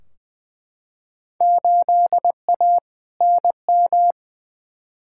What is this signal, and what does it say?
Morse code, heard as a keyed tone, about 21 wpm, 700 Hz. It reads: 8A NM
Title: Field Day Exchanges Encore - 20wpm